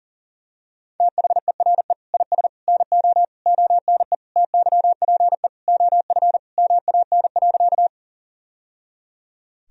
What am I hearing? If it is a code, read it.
Morse code, heard as a keyed tone, about 37 wpm, 700 Hz. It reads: THERE IS NO ONE TYPE OF MAN.
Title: Sentences from Top 1000 Words - Review Part 3 - 40wpm